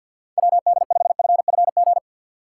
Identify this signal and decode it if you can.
Morse code, heard as a keyed tone, about 52 wpm, 700 Hz. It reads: WD5FFG
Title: USA Call Signs - 6 Characters - 4x Character Spacing - 50wpm